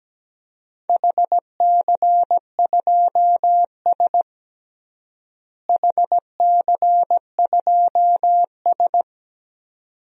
Morse code, heard as a keyed tone, about 17 wpm, 700 Hz. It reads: HC2S HC2S
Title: DXCC-100 Call Signs - 4 Characters - Repeated 2x - 17wpm